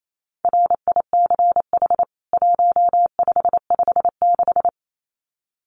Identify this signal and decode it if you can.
Morse code, heard as a keyed tone, about 28 wpm, 700 Hz. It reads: RICH 1556
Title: CWT Exchange 28wpm